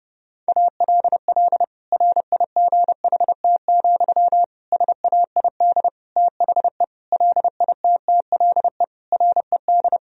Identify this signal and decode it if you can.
Morse code, heard as a keyed tone, about 30 wpm, 700 Hz. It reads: ALL RIGHT, SAID THE LITTLE RED
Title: Sentences from Top 1000 Words - Review Part 2 - 30wpm